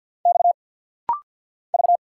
Morse code, heard as a keyed tone, about 49 wpm, 700 Hz. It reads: X V
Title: Single Letter-Number - Rapid-Fire 50wpm